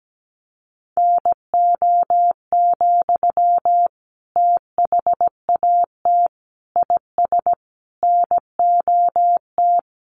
Morse code, heard as a keyed tone, about 17 wpm, 700 Hz. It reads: NO, THAT IS NOT
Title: Sentences from Top 600 Words 17wpm